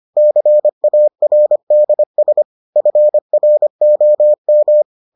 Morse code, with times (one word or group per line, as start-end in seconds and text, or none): 0.2-2.4 CARDS
2.8-4.8 FROM